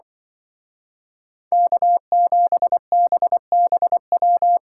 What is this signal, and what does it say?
Morse code, heard as a keyed tone, about 23 wpm, 700 Hz. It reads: K7BBW
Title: Speed Racing - USA 1x3 Call Signs 20wpm